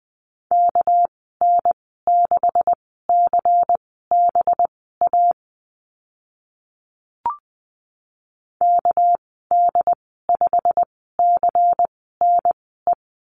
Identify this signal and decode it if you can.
Morse code, heard as a keyed tone, about 19 wpm, spaced as for 14 wpm, 700 Hz. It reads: KN6CBA KD5CNE
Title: 200 Lesson 32g - Letter k - Call Signs 20wpm